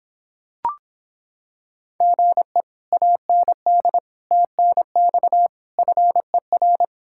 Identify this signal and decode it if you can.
Morse code, heard as a keyed tone, about 26 wpm, 700 Hz. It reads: GE AND TNX FER